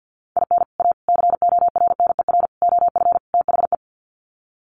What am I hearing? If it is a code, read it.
Morse code, heard as a keyed tone, about 50 wpm, 700 Hz. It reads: IN A CORNER OF THE
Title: Sets of 5 Words Encore2 50wpm